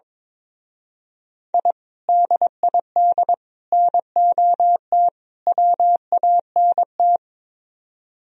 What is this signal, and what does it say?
Morse code, heard as a keyed tone, about 22 wpm, 700 Hz. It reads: I DID NOT WANT
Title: Speed-Racing - Sets of 4 Words 22wpm